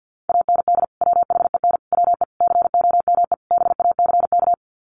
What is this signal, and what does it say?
Morse code, heard as a keyed tone, about 50 wpm, 700 Hz. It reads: AND WHEN WE COME BACK
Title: Sets of 5 Words 50wpm